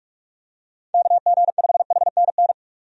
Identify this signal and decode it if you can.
Morse code, heard as a keyed tone, about 43 wpm, 700 Hz. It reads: KG5HNN
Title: USA Call Signs - 6 Characters - 45wpm